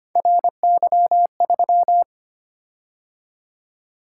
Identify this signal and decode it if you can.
Morse code, heard as a keyed tone, about 25 wpm, 700 Hz. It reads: RY3
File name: Three Letter-Number Combinations - 25wpm